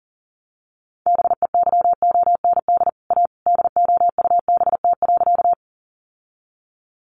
Morse code, heard as a keyed tone, about 40 wpm, 700 Hz. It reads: BEYOND A DOUBT.
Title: Sentences from Top 1500 Words 40wpm